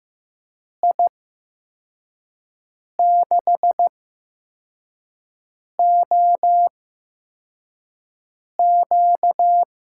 Morse code, heard as a keyed tone, about 15 wpm, spaced as for 4 wpm, 700 Hz. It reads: I6OQ
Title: USA Call Signs - 6 Characters - 8x Character Spacing - 15wpm